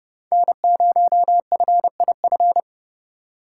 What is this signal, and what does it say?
Morse code, heard as a keyed tone, about 29 wpm, 700 Hz. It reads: N0FIF